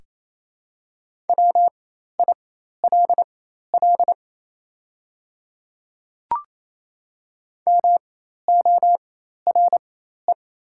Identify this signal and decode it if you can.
Morse code, heard as a keyed tone, about 27 wpm, spaced as for 13 wpm, 700 Hz. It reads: WILL MORE
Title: 4x Character Spacing - Top 200 Words - 28wpm